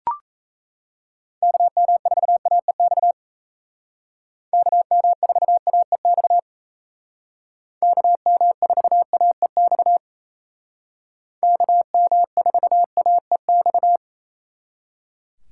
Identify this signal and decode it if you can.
Morse code, heard as a keyed tone, about 37 wpm, 700 Hz. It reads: KM4AEX KM4AEX KM4AEX KM4AEX